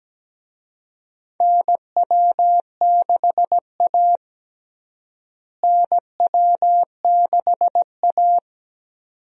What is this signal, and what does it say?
Morse code, heard as a keyed tone, about 17 wpm, 700 Hz. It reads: NW6A NW6A
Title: USA Call Signs - 4 Characters - Repeated 2x - 17wpm